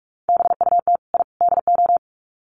Morse code, heard as a keyed tone, about 44 wpm, 700 Hz. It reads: BUT I DO
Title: Sets of 3 Words 45wpm